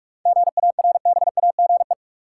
Morse code, heard as a keyed tone, about 43 wpm, 700 Hz. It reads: GARBAGE
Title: Seven Letter Words - 45wpm